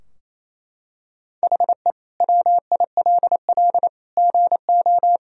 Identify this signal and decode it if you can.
Morse code, heard as a keyed tone, about 27 wpm, 700 Hz. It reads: HE WILL GO
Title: Sentences from Top 1000 Words - Review Part 2 - 28wpm